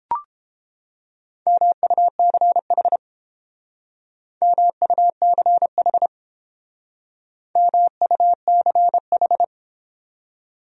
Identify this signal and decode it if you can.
Morse code, heard as a keyed tone, about 30 wpm, 700 Hz. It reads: MUCH MUCH MUCH